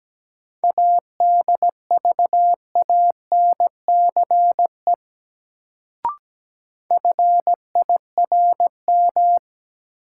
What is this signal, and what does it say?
Morse code, heard as a keyed tone, about 17 wpm, 700 Hz. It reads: ADVANCE FIRM
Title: Sentences from Top 1400 Words - New Word Drill - 17wpm